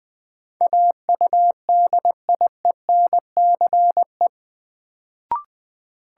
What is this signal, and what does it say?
Morse code, heard as a keyed tone, about 20 wpm, 700 Hz. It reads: AUDIENCE
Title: Sentences from Top 800 Words - New Word Drill - 20wpm